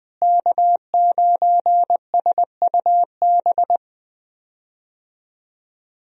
Morse code, heard as a keyed tone, about 20 wpm, 700 Hz. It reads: K9SUB
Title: USA Call Signs - 5 Characters - Repeated 2x - 20wpm